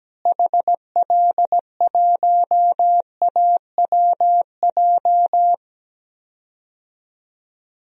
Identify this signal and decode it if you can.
Morse code, heard as a keyed tone, about 17 wpm, 700 Hz. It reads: HL1AWJ
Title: DXCC-100 Call Signs - 6 Characters - Repeated 2x - 17wpm